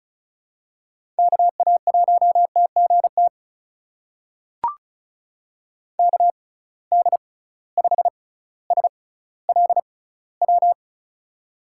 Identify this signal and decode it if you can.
Morse code, heard as a keyed tone, about 36 wpm, spaced as for 18 wpm, 700 Hz. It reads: KA1TGT KD5SLW
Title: USA Call Signs - 6 Characters - 6x Character Spacing - 35wpm